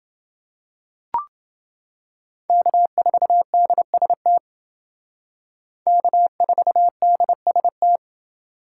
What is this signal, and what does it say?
Morse code, heard as a keyed tone, about 28 wpm, 700 Hz. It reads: K4DST K4DST